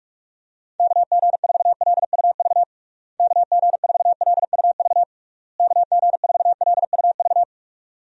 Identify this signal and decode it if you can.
Morse code, heard as a keyed tone, about 46 wpm, 700 Hz. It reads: KG4LUV KG4LUV KG4LUV
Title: USA Call Signs - 6 Characters - Repeated 3x - 45wpm